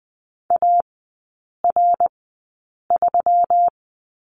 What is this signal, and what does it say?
Morse code, heard as a keyed tone, about 20 wpm, spaced as for 8 wpm, 700 Hz. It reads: AR3
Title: 138 Lesson 23e - Letter f - Three Characters 20wpm